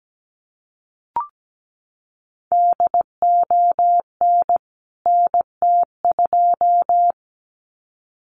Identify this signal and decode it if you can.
Morse code, heard as a keyed tone, about 17 wpm, 700 Hz. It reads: DON NT2
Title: CWT Exchange 17wpm